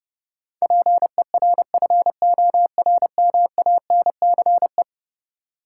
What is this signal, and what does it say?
Morse code, heard as a keyed tone, about 30 wpm, 700 Hz. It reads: PERFORMANCE
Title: Sentences from Top 1000 Words - New Word Drill - 30wpm